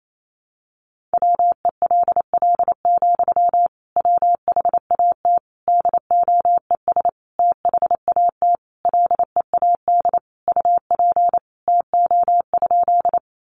Morse code, heard as a keyed tone, about 28 wpm, 700 Hz. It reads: WELL, WHAT DOES THAT LEAD UP TO?
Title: Sentences from Top 500 Words 28wpm